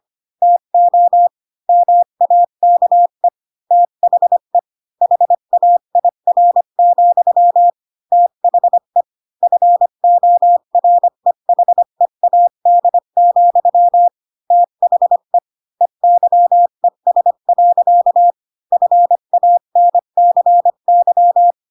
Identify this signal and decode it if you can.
Morse code, heard as a keyed tone, about 25 wpm, 700 Hz. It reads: TO MAKE THE HAIR, THE FOREHEAD, THE EYES. FANCY
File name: Pinocchio / Pinocchio ch3 20 25 ik1hge